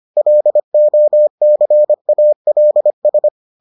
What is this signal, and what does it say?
Morse code, heard as a keyed tone, about 25 wpm, 600 Hz. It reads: LOCALS